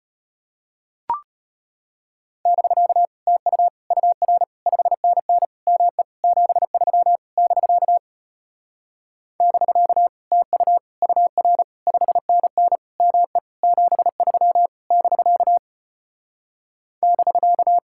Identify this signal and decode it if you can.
Morse code, heard as a keyed tone, about 33 wpm, 700 Hz. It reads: <BK> TU UR 5NN ME 73 <BK> <BK> TU UR 5NN ME 73 <BK> <BK>